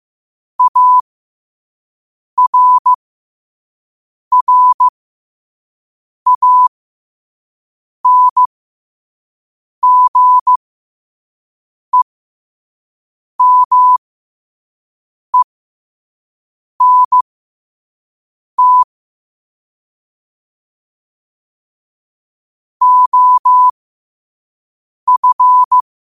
Morse code, heard as a keyed tone, about 15 wpm, spaced as for 5 wpm, 1000 Hz. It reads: ARRANGEMENT OF
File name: qotdfiles / 5wpm